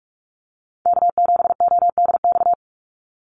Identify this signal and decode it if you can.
Morse code, heard as a keyed tone, about 46 wpm, 700 Hz. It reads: K7ODX
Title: NA SOTA Chaser Callsigns 45wpm